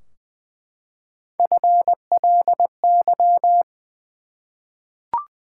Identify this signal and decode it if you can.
Morse code, heard as a keyed tone, about 20 wpm, 700 Hz. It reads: FLY
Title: Three Letter Words - 20wpm